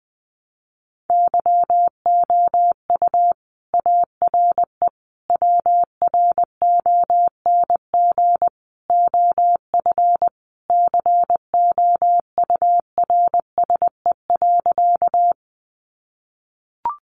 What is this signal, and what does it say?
Morse code, heard as a keyed tone, about 20 wpm, 700 Hz. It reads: YOU ARE WRONG OF COURSE.
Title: Sentences from Top 900 Words 20wpm